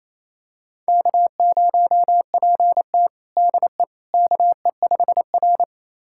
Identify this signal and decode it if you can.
Morse code, heard as a keyed tone, about 29 wpm, 690 Hz. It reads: K0PT DE KE5R
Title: USA Callsign DE Callsign - 4 Characters - 28wpm